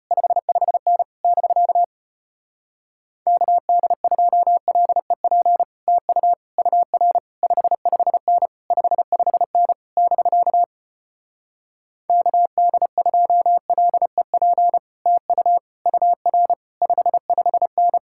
Morse code, heard as a keyed tone, about 33 wpm, 700 Hz. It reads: 55N <BK> KD2LEP TU UR 55N 55N <BK> KD2LEP TU UR 55N